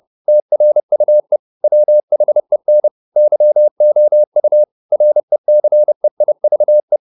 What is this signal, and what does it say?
Morse code, heard as a keyed tone, about 30 wpm, 600 Hz. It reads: TRUE WHEN YOU RECEIVE